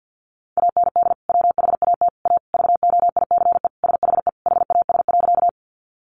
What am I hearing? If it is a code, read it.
Morse code, heard as a keyed tone, about 50 wpm, 700 Hz. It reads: AND WHAT A VOICE SHE HAS.
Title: Sentences from Top 500 Words 50wpm